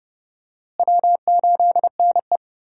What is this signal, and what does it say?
Morse code, heard as a keyed tone, about 30 wpm, 700 Hz. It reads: W8NE
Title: USA Call Signs - 4 Characters - Repeated 3x - 30wpm